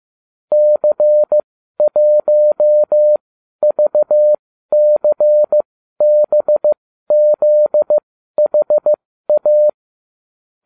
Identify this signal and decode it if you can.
Morse code, heard as a keyed tone, about 15 wpm, spaced as for 12 wpm, 600 Hz. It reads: C1VCBZHA